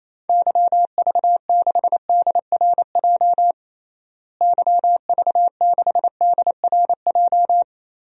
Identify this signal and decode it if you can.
Morse code, heard as a keyed tone, about 28 wpm, 700 Hz. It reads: YV6DRJ YV6DRJ
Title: DXCC-100 Call Signs - 6 Characters - Repeated 2x - 28wpm